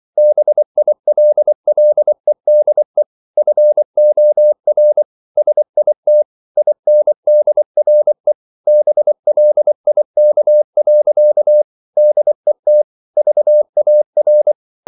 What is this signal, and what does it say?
Morse code, heard as a keyed tone, about 24 wpm, 600 Hz. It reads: BILLEDE FOR SIT INDRE BLIK. DET VAR
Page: Ferie 24 wpm